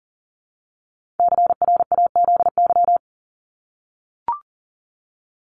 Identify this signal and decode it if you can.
Morse code, heard as a keyed tone, about 40 wpm, 700 Hz. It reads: CRAZY
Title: Top 1501-1750 Words 40wpm